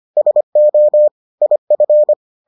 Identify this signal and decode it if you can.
Morse code, heard as a keyed tone, about 25 wpm, 600 Hz. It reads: SO IF